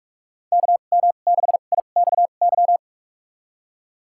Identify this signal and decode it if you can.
Morse code, heard as a keyed tone, about 47 wpm, spaced as for 33 wpm, 700 Hz. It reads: KM6IXY
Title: USA Call Signs - 6 Characters - 2x Character Spacing - 45wpm